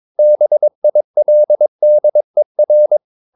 Morse code, heard as a keyed tone, about 22 wpm, 600 Hz. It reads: BILDER
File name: Ferie QRS 22wpm